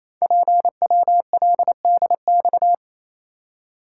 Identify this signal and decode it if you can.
Morse code, heard as a keyed tone, about 29 wpm, 700 Hz. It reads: PWLDX
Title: Code Groups - 5 Letter:Numbers - 28wpm